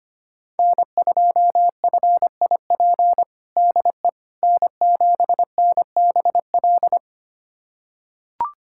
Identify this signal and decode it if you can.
Morse code, heard as a keyed tone, about 25 wpm, 700 Hz. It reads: N2FIP DE N7NBL
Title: USA Callsign DE Callsign - 5 Characters - 25wpm